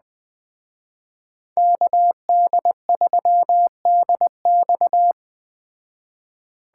Morse code, heard as a keyed tone, about 20 wpm, 700 Hz. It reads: KD3DX
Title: Speed Racing - USA 2x2 Call Signs 17wpm